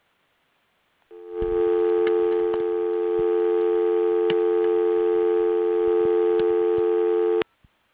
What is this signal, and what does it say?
Dial tone (North America), 350 + 440 Hz.